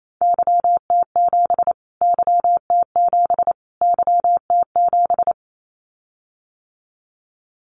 Morse code, heard as a keyed tone, about 28 wpm, 700 Hz. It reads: YT7 YT7 YT7
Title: Callsign Prefixes for Top 25 DXCC - Repeated 3x - 28wpm